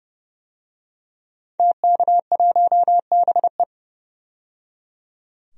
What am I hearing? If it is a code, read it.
Morse code, heard as a keyed tone, about 30 wpm, 700 Hz. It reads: TK1BE
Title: DXCC-100 Call Signs - 5 Characters - 30wpm